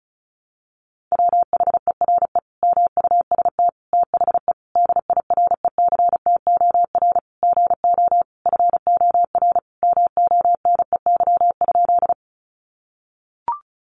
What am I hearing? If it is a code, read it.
Morse code, heard as a keyed tone, about 34 wpm, 700 Hz. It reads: WHERE MUST THE DIRECTOR GO FOR MONEY?
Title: Sentences from Top 1300 Words 35wpm